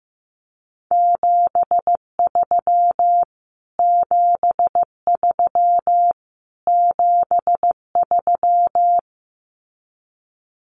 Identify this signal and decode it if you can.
Morse code, heard as a keyed tone, about 15 wpm, 700 Hz. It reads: 73 73 73
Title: Abbreviations Repeated 3x - 15wpm